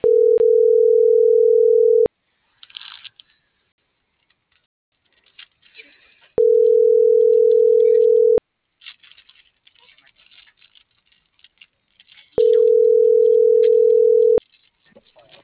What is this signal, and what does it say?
Ringback tone (North America), 440 + 480 Hz.